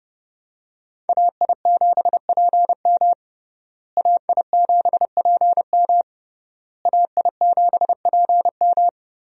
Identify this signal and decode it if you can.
Morse code, heard as a keyed tone, about 30 wpm, 700 Hz. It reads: AI7PM AI7PM AI7PM
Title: USA Call Signs - 5 Characters - Repeated 3x - 30wpm